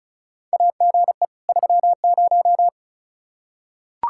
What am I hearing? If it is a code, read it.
Morse code, heard as a keyed tone, about 36 wpm, 700 Hz. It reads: AGE 30
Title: Practice QSO - Exchange 3 - More Info - 35wpm